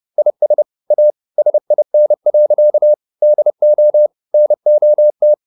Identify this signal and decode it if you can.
Morse code, heard as a keyed tone, about 30 wpm, 600 Hz. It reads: IS A SIN. DO NOT